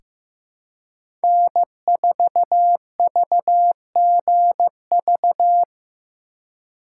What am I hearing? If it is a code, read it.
Morse code, heard as a keyed tone, about 15 wpm, 700 Hz. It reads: N4VGV
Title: USA Call Signs - 4x Character Spacing - 15wpm